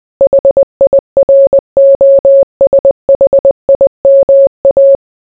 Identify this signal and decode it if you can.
Morse code, heard as a keyed tone, about 20 wpm, 550 Hz. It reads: HIROSHIMA